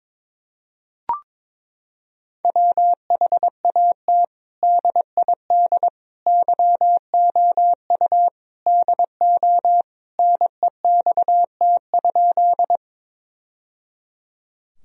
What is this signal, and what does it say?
Morse code, heard as a keyed tone, about 22 wpm, 700 Hz. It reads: WHAT DID YOU DO NEXT?
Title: Sentences from Top 1000 Words - Review Part 1 - 22wpm